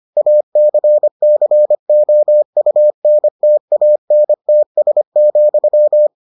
Morse code, heard as a keyed tone, about 25 wpm, 600 Hz. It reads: ACCOUNTANTS,